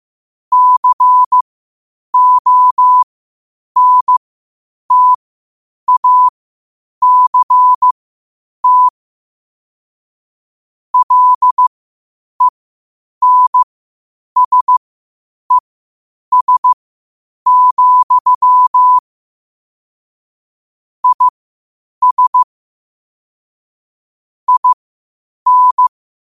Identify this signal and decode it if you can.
Morse code, heard as a keyed tone, about 15 wpm, spaced as for 8 wpm, 1000 Hz. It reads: CONTACT LENSES, IS IN
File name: qotdfiles / 8wpm